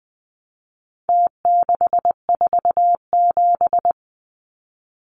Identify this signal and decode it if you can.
Morse code, heard as a keyed tone, about 20 wpm, 700 Hz. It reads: T647
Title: Numbers - Contest Style - 4-Digit 20wpm